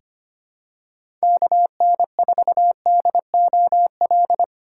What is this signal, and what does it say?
Morse code, heard as a keyed tone, about 25 wpm, 700 Hz. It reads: KN4DOL
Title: USA Call Signs - 6x Character Spacing - 25wpm